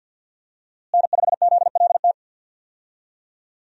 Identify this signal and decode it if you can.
Morse code, heard as a keyed tone, about 52 wpm, 700 Hz. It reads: N5ZLT
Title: USA Call Signs - 5 Characters - Repeated 2x - 50wpm